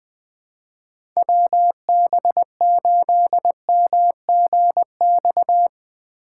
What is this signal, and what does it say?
Morse code, heard as a keyed tone, about 20 wpm, 700 Hz. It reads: WB8MGX
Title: USA Call Signs - 6 Characters - 4x Character Spacing - 20wpm